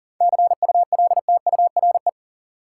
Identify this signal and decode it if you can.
Morse code, heard as a keyed tone, about 40 wpm, 700 Hz. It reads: CULTURE
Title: Sentences from Top 1000 Words - Review Part 6 - New Word Drill - 40wpm